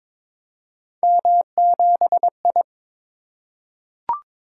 Morse code, heard as a keyed tone, about 22 wpm, 700 Hz. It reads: M7I
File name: practice / 22wpm / Three Letter-Number Combinations - 22wpm